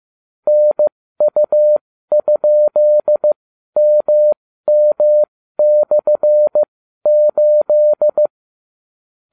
Morse code, heard as a keyed tone, about 15 wpm, spaced as for 12 wpm, 600 Hz. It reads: NU?MM/8
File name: cw / ubacw31 1